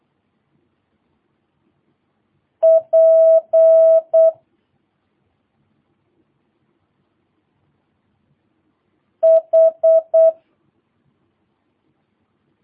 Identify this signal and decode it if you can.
Morse code, heard as a keyed tone, about 8 wpm, 650 Hz. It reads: P H
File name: jh8cbh / 2022cw